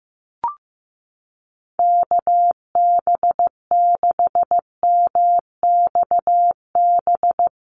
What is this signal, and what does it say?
Morse code, heard as a keyed tone, about 15 wpm, 700 Hz. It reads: KB6MXB
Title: USA Call Signs Repeated 2x 15wpm